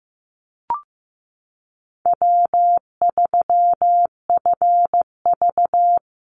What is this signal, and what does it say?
Morse code, heard as a keyed tone, about 15 wpm, 700 Hz. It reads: W3FV
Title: Easy Call Signs Repeated 3x - 15wpm